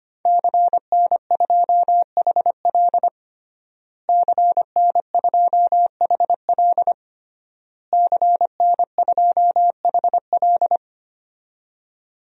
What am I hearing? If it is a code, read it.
Morse code, heard as a keyed tone, about 25 wpm, 700 Hz. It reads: CN2HL CN2HL CN2HL
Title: DXCC-100 Call Signs - 5 Characters - Repeated 3x - 25wpm